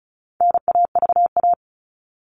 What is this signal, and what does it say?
Morse code, heard as a keyed tone, about 34 wpm, 700 Hz. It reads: NAVA
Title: All Cut Numbers - 4-Digit 35wpm